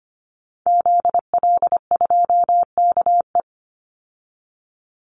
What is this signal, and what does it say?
Morse code, heard as a keyed tone, about 25 wpm, 700 Hz. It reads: ZL2KE
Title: Call Signs Repeated 2x - 25wpm